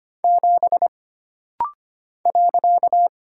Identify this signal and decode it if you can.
Morse code, heard as a keyed tone, about 25 wpm, 700 Hz. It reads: 7 .
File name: Single Character (Letter:Number:Punctuation:Prosign) - Rapid-Fire - 25wpm